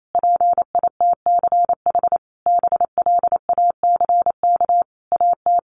Morse code, heard as a keyed tone, about 28 wpm, 700 Hz. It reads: PITCH BLACK AT
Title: Sentences from Top 1300 Words 28wpm